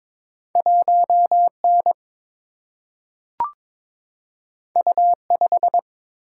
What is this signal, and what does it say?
Morse code, heard as a keyed tone, about 22 wpm, 690 Hz. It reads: 1N U5
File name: practice / 22wpm / Two Letter-Number Combinations - 22wpm